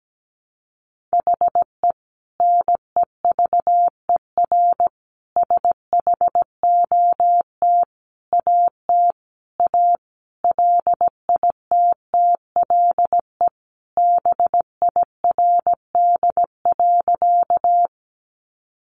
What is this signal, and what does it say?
Morse code, heard as a keyed tone, about 17 wpm, 700 Hz. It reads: HE NEVER SHOT AT A LITTLE BIRD.
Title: Sentences from Top 1100 Words 17wpm